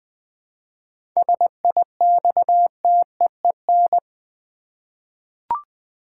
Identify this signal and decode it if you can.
Morse code, heard as a keyed tone, about 20 wpm, 700 Hz. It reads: SIXTEEN
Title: 7 Letter Words - First 3 Letters Given - 20wpm